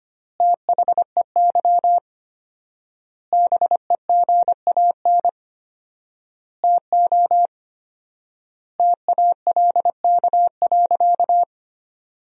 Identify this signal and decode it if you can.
Morse code, heard as a keyed tone, about 25 wpm, 700 Hz. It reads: THEY BEGAN TO TALK.
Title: 4x Word-Spacing - Sentences from Top 400 Words - 25wpm